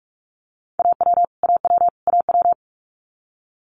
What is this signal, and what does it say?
Morse code, heard as a keyed tone, about 46 wpm, 700 Hz. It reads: AW AW AW
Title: Multi-letter Phonemes Repeated 3x 45wpm